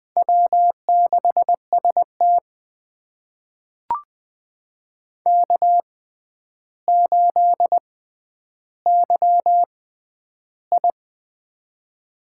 Morse code, heard as a keyed tone, about 20 wpm, spaced as for 10 wpm, 700 Hz. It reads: W6ST K8YI
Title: USA Call Signs - 6x Character Spacing - 20wpm